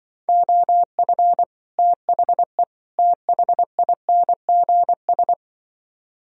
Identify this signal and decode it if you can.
Morse code, heard as a keyed tone, about 25 wpm, 700 Hz. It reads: OF THE THINGS
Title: Speed-Racing - Sets of 3 Words 20wpm